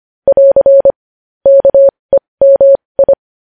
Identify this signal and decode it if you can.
Morse code, heard as a keyed tone, about 25 wpm, spaced as for 20 wpm, 550 Hz. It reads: <AR> KEMI
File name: podcast / News headlines-1x-25x20